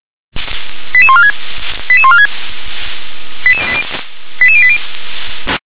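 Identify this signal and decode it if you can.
Five-tone selective call (ZVEI), heard as quick runs of five tones, tones of about 70 ms. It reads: 88146 88146 88080
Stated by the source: ZVEI five-tone selective call 88742 received by SDR.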